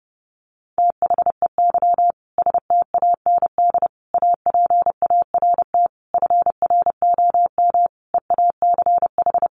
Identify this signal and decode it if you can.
Morse code, heard as a keyed tone, about 30 wpm, 700 Hz. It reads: THEY STAND APART FROM EACH
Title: Easy Phrases 30wpm